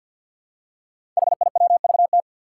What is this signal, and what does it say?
Morse code, heard as a keyed tone, about 48 wpm, 700 Hz. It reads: HIWVT
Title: Five Challenging Letter-Number Combinations 50wpm